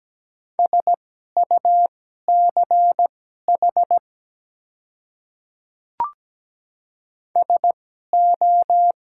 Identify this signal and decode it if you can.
Morse code, heard as a keyed tone, about 17 wpm, spaced as for 12 wpm, 700 Hz. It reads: SUCH SO
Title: 2x Character Spacing - Top 200 Words - 17wpm